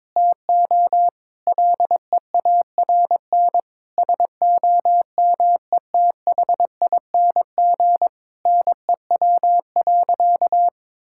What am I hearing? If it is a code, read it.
Morse code, heard as a keyed tone, about 22 wpm, 700 Hz. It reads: TO LEARN SOMETHING NEW.
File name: Difficult Sentences with Hints - 22wpm